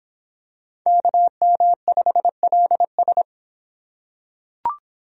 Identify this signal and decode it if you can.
Morse code, heard as a keyed tone, about 25 wpm, 700 Hz. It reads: KM5LS